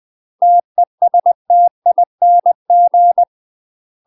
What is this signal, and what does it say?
Morse code, heard as a keyed tone, about 20 wpm, 700 Hz. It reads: TESTING
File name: UK MASTER QSOs TRAINING / 20WPM / UK MASTER QSOs 0010